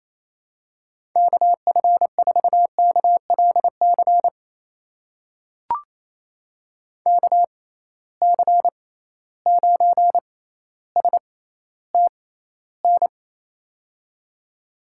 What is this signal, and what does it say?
Morse code, heard as a keyed tone, about 28 wpm, spaced as for 14 wpm, 700 Hz. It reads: KF4KLC KC9STN
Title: USA Call Signs - 6 Characters - 6x Character Spacing - 28wpm